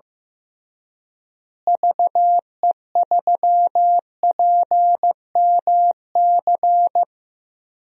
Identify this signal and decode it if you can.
Morse code, heard as a keyed tone, about 15 wpm, 700 Hz. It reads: VE3PMC
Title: DXCC-100 Call Signs - 6 Characters - Repeated 3x - 15wpm